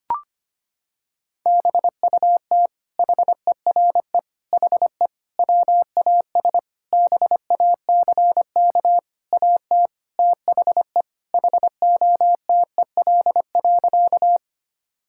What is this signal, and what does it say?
Morse code, heard as a keyed tone, about 25 wpm, 700 Hz. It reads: BUT HERE HE WAS BACK AT THE HOTEL.
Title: Sentences from Top 1600 Words 25wpm